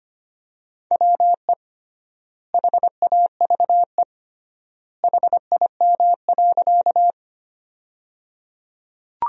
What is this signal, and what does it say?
Morse code, heard as a keyed tone, about 25 wpm, 700 Hz. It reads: WE HAVE HIM.
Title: 3x Word-Spacing - Sentences from Top 100 Words - 25wpm